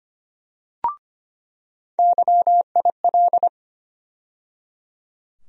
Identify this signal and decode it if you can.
Morse code, heard as a keyed tone, about 25 wpm, 700 Hz. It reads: YIL
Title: USA Call Sign Postfixes - 3 Letters 25wpm